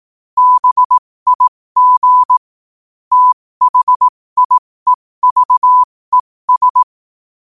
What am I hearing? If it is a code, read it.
Morse code, heard as a keyed tone, about 18 wpm, spaced as for 15 wpm, 1000 Hz. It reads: BIG THIEVES
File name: qotdfiles / 15wpm